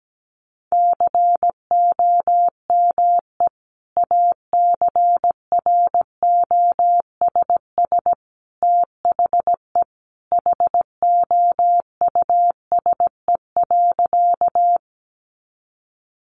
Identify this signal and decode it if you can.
Morse code, heard as a keyed tone, about 17 wpm, 700 Hz. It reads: COME ACROSS THE HOUSE.
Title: Sentences from Top 700 Words 17wpm